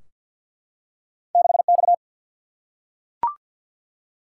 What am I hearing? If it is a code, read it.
Morse code, heard as a keyed tone, about 50 wpm, 700 Hz. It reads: 6 X
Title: Single-Letter Single-Number 50wpm